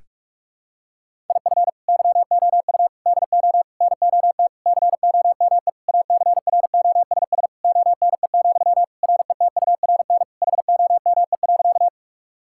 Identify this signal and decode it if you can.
Morse code, heard as a keyed tone, about 46 wpm, 700 Hz. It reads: IF YOU DO NOT COME ACROSS ONE, RETURN HOME.
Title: Sentences from Top 1000 Words - Review Part 5 - 45wpm